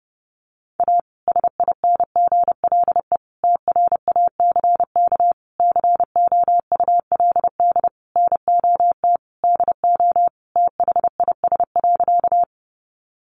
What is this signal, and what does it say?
Morse code, heard as a keyed tone, about 30 wpm, 700 Hz. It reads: A SINGLE TRACK COULD NOT DO THIS.
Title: Sentences from Top 1100 Words 30wpm